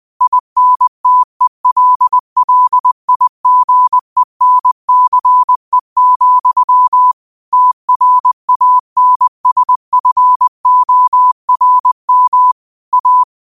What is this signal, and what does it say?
Morse code, heard as a keyed tone, about 20 wpm, 1000 Hz. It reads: INTELLIGENCE, TRANSFORM A